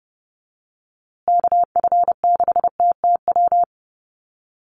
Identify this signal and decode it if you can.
Morse code, heard as a keyed tone, about 30 wpm, 700 Hz. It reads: KF6TTW
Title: USA Call Signs Repeated 3x 30wpm